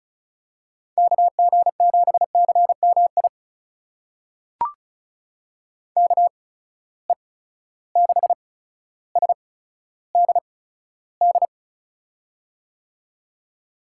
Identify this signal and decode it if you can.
Morse code, heard as a keyed tone, about 36 wpm, spaced as for 15 wpm, 700 Hz. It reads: KG7CMI KE6SDD
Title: USA Call Signs - 6 Characters - 8x Character Spacing - 35wpm